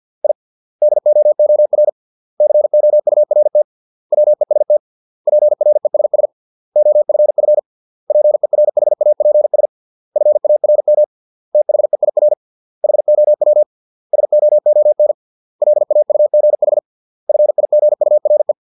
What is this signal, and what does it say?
Morse code, heard as a keyed tone, about 50 wpm, 600 Hz. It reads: I DOOR COURT WEST PRESS OFF PERHAPS FARM THEIR HOW SOON LAUGH FIGURE